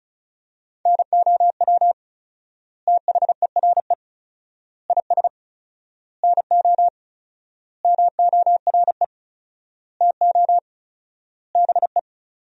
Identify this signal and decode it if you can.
Morse code, heard as a keyed tone, about 36 wpm, 700 Hz. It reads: NOW THERE IS NO MORE TO BE
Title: 4x Word-Spacing - Sentences from Top 100 Words - 35wpm